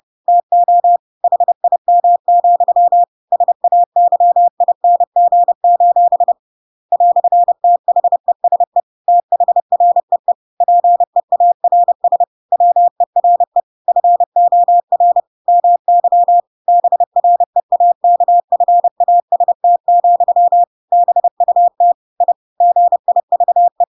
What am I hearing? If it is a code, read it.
Morse code, heard as a keyed tone, about 30 wpm, 700 Hz. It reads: TO HIM, SAYING: "THESE THREE PEARS WERE FOR MY BREAKFAST, BUT I GIVE
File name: Pinocchio / Pinocchio ch7 30 30 ik1hge